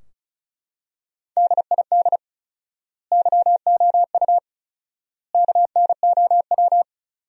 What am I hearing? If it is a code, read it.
Morse code, heard as a keyed tone, about 34 wpm, 700 Hz. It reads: DID YOU KNOW
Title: 4x Word-Spacing - Sentences from Top 100 Words - 35wpm